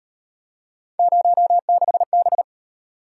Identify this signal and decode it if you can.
Morse code, heard as a keyed tone, about 35 wpm, 700 Hz. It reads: 06B